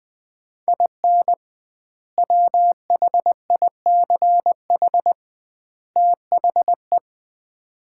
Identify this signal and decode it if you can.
Morse code, heard as a keyed tone, about 20 wpm, 700 Hz. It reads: IN WHICH THE
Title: 2x Word-Spacing - Sets of 3 Words - 20wpm